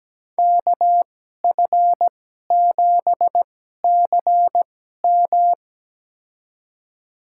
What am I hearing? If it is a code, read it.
Morse code, heard as a keyed tone, about 17 wpm, spaced as for 12 wpm, 700 Hz. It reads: KF7CM
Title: USA Call Signs - 5 Characters - 2x Character Spacing - 17wpm